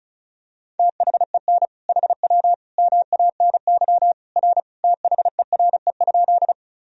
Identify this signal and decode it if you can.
Morse code, heard as a keyed tone, about 36 wpm, 700 Hz. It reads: THEN HW MANY R THERE?
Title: Sentences from Top 100 Words and Abbreviations 35wpm